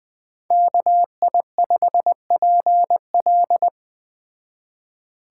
Morse code, heard as a keyed tone, about 20 wpm, 700 Hz. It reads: KI5PL
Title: Speed Racing - USA 2x2 Call Signs 17wpm